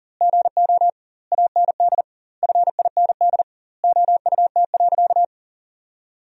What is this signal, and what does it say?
Morse code, heard as a keyed tone, about 37 wpm, 700 Hz. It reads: GO AND FIND OUT.
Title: 2x Word-Spacing - Sentences from Top 200 Words - 40wpm